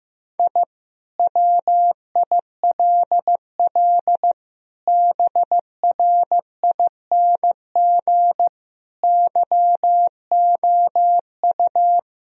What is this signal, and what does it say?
Morse code, heard as a keyed tone, about 15 wpm, 700 Hz. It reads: I WILL BRING YOU
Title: Sentences from Top 1000 Words - Review Part 3 - 15wpm